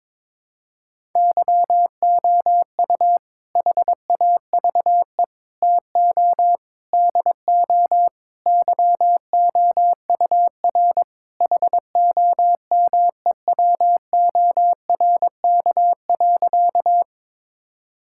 Morse code, heard as a keyed tone, about 22 wpm, 700 Hz. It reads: YOU HAVE TO DO YOUR HOMEWORK.
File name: practice / 22wpm / Sentences with 4-Word Phrases - 22wpm